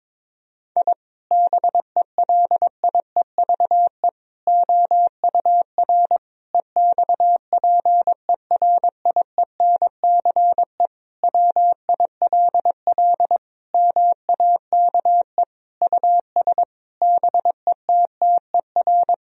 Morse code, heard as a keyed tone, about 22 wpm, 700 Hz. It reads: I BELIEVE OUR EXPERIENCE WILL MAKE US BETTER
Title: Sentences from Top 1000 Words - Review Part 3 - 22wpm